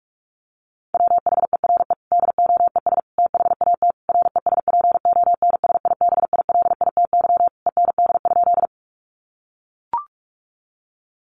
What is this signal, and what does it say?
Morse code, heard as a keyed tone, about 44 wpm, 700 Hz. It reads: WHERE DOES THAT RESPONSIBILITY END?
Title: Sentences from Top 1200 Words 45wpm